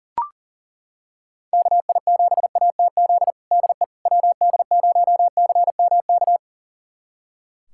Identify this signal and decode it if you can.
Morse code, heard as a keyed tone, about 40 wpm, 700 Hz. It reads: KI7ATZ DE WD0CMK
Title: USA Callsign DE Callsign - 6 Characters - 40wpm